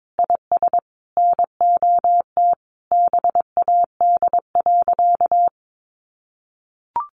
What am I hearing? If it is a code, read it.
Morse code, heard as a keyed tone, about 22 wpm, 700 Hz. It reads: IS NOT BAD.
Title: Sentences from Top 700 Words 22wpm